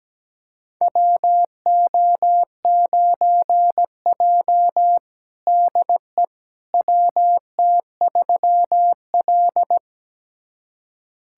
Morse code, heard as a keyed tone, about 17 wpm, 700 Hz. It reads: WO9J DE WT3L
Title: USA Callsign DE Callsign - 4 Characters - 17wpm